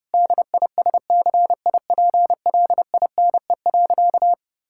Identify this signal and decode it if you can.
Morse code, heard as a keyed tone, about 32 wpm, 700 Hz. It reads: DISCIPLINE.
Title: Sentences from Top 1000 Words - Review Part 6 - 30wpm